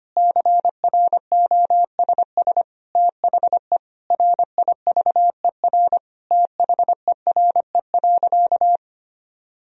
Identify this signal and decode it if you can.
Morse code, heard as a keyed tone, about 25 wpm, 700 Hz. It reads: CROSS THE RIVER THERE.
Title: Sentences from Top 1000 Words - Review Part 3 - 25wpm